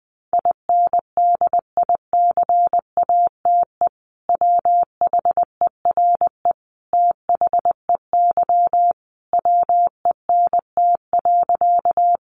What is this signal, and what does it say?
Morse code, heard as a keyed tone, about 20 wpm, 700 Hz. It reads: INDICATE WHERE THEY WENT.
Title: Sentences from Top 1300 Words 20wpm